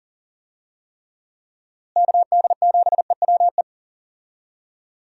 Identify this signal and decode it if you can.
Morse code, heard as a keyed tone, about 40 wpm, 700 Hz. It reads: KD7EWE
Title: USA Call Signs - 6 Characters - Repeated 3x - 40wpm